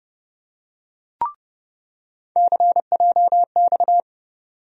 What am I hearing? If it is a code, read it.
Morse code, heard as a keyed tone, about 32 wpm, 700 Hz. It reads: CJX
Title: USA Call Sign Postfixes - 3 Letters 30wpm